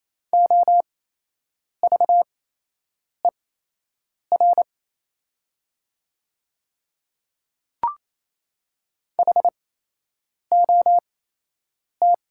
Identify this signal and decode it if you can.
Morse code, heard as a keyed tone, about 28 wpm, spaced as for 8 wpm, 700 Hz. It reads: OVER HOT
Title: 8x Character Spacing - Top 100 Words - 28wpm